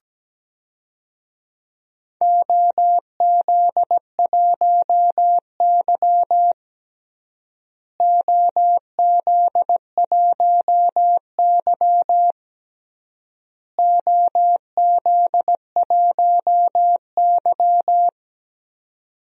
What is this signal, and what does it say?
Morse code, heard as a keyed tone, about 17 wpm, 700 Hz. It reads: OZ1Y OZ1Y OZ1Y
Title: DXCC-100 Call Signs - 4 Characters - Repeated 3x - 17wpm